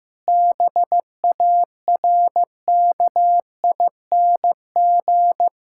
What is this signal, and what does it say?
Morse code, heard as a keyed tone, about 16 wpm, 700 Hz. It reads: BARKING
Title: Pangram Challenge v2 - 15wpm